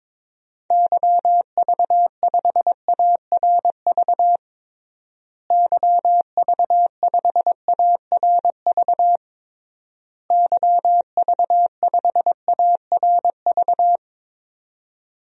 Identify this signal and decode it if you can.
Morse code, heard as a keyed tone, about 22 wpm, 700 Hz. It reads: YV5ARV YV5ARV YV5ARV
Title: DXCC-100 Call Signs - 6 Characters - Repeated 3x - 22wpm